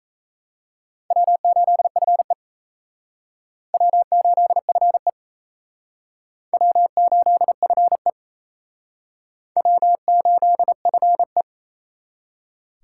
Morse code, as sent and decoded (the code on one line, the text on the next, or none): .-- ---.. ..-. / .-- ---.. ..-. / .-- ---.. ..-. / .-- ---.. ..-. .
W8FE W8FE W8FE W8FE